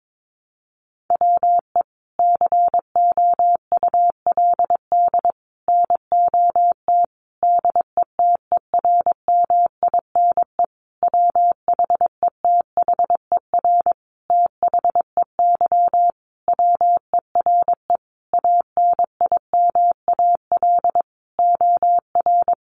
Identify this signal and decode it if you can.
Morse code, heard as a keyed tone, about 22 wpm, 700 Hz. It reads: WE COULD NOT DETERMINE WHETHER THEY WERE ANIMAL OR
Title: Sentences from Top 1300 Words 22wpm